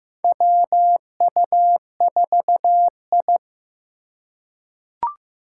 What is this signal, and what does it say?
Morse code, heard as a keyed tone, about 15 wpm, 700 Hz. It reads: WU4I
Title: USA Call Signs - 4 Characters - 2x Character Spacing - 15wpm